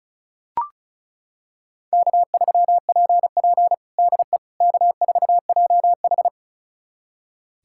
Morse code, heard as a keyed tone, about 36 wpm, 700 Hz. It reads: K3PP DE K4JH
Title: USA Callsign DE Callsign - 4 Characters - 35wpm